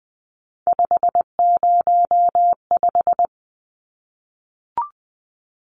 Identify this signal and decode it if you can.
Morse code, heard as a keyed tone, about 20 wpm, 700 Hz. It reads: 505
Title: Numbers - 3-Digit 20wpm